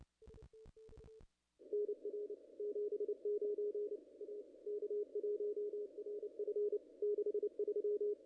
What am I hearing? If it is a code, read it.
Morse code, heard as a keyed tone, about 28 wpm, 440 Hz. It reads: NR 79 A K1RF 63
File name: contests / ss-cw-k5tr-op-n5rz -0000